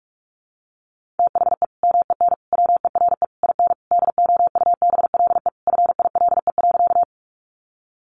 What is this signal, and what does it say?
Morse code, heard as a keyed tone, about 44 wpm, 700 Hz. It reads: THE MEN WERE IN DOUBLE FILE.
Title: Sentences from Top 1400 Words 45wpm